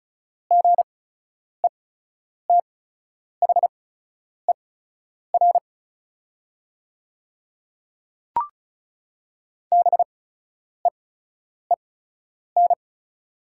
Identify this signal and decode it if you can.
Morse code, heard as a keyed tone, about 34 wpm, spaced as for 10 wpm, 700 Hz. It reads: GETHER BEEN
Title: 8x Character Spacing - Top 300 Words - 35wpm